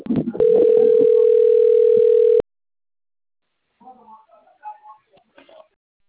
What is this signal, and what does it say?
Ringback tone (North America), 440 + 480 Hz.
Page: Americana - SP